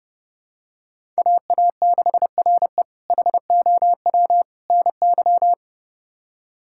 Morse code, heard as a keyed tone, about 30 wpm, 700 Hz. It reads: AA6RE HOW NY
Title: K1USN SST QSO Practice - 30wpm